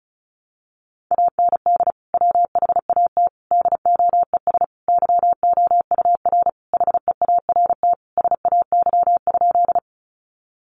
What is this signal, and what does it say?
Morse code, heard as a keyed tone, about 34 wpm, 700 Hz. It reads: AND WHAT DOES YOUR HEART SAY?
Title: Sentences from Top 1200 Words 35wpm